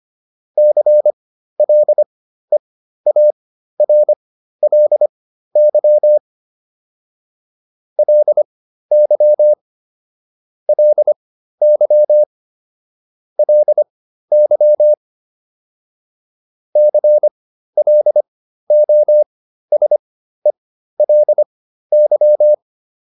Morse code, heard as a keyed tone, about 25 wpm, spaced as for 13 wpm, 600 Hz. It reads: CLEARLY LY LY LY CLOSELY
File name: LY7 13